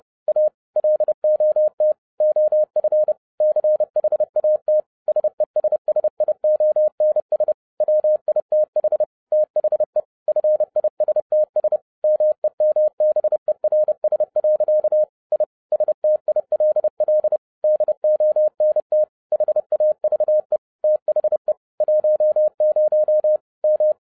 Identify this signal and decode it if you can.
Morse code, heard as a keyed tone, about 30 wpm, 610 Hz. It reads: A LOT OF CHAT SESSIONS WITH THE FISTS MEMBERS. I STILL DONT HAVE THE 10 M
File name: 30wpm